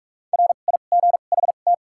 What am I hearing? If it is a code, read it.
Morse code, heard as a keyed tone, about 43 wpm, spaced as for 32 wpm, 700 Hz. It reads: RIGHT
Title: 2x Character Spacing - Top 500 Words - 45wpm